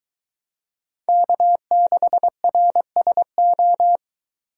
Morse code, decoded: K6RSO

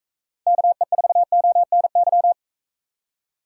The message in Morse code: KE4ONY

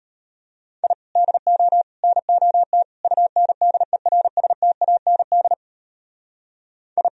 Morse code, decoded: I DO NOT UNDERSTAND I

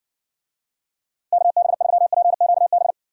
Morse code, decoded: KB3PQB

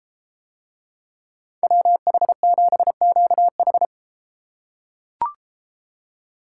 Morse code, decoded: WH7QH

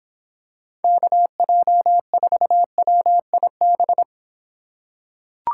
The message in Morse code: KJ4WIB

